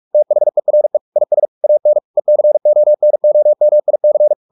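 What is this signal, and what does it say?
Morse code, heard as a keyed tone, about 44 wpm, 600 Hz. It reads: THERE IS AN ECONOMIC
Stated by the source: Morse practice at 45 wpm, the novel Never by Ken Follett in English, from CW med Gnister.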